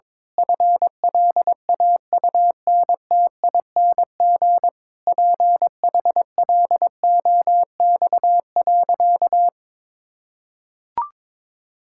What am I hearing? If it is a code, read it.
Morse code, heard as a keyed tone, about 23 wpm, 700 Hz. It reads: FLAUNTING PHLOX.
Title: Pangram Challenge v2 - 22wpm